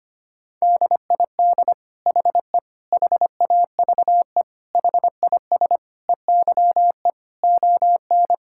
Morse code, heard as a keyed tone, about 25 wpm, 700 Hz. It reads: DID HE HAVE HIS EYE ON
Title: Sentences from Top 1000 Words - Review Part 5 - 25wpm